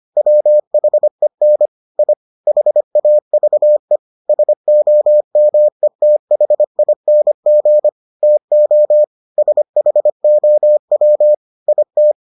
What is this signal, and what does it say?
Morse code, heard as a keyed tone, about 25 wpm, 600 Hz. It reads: WHEN I HAVE SOMETHING TO SHOW IT